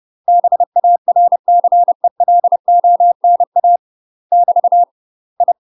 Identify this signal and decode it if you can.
Morse code, heard as a keyed tone, about 30 wpm, 700 Hz. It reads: BARCELONA <BT> I
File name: UK MASTER QSOs TRAINING / 30WPM / UK MASTER QSOs 0049